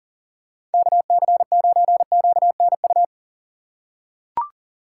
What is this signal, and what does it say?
Morse code, heard as a keyed tone, about 40 wpm, 700 Hz. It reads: KC9QNU
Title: USA Call Signs - 6 Characters - 4x Character Spacing - 40wpm